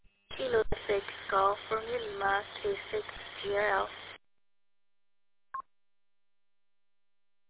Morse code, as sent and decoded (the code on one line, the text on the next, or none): none